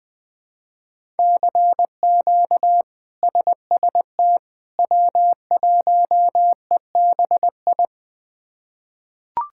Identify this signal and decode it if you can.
Morse code, heard as a keyed tone, about 21 wpm, 700 Hz. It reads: CQ SST W1EBI
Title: K1USN SST QSO Practice - 20wpm